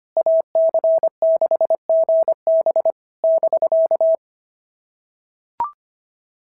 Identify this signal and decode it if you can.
Morse code, heard as a keyed tone, about 25 wpm, 650 Hz. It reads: AC6GB <BK>
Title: SOTA QSO Practice 25wpm